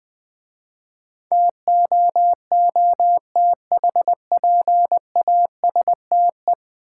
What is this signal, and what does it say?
Morse code, heard as a keyed tone, about 20 wpm, 700 Hz. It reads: TOOTHPASTE
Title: Easy Words - Speed-Racing with Hints - 20wpm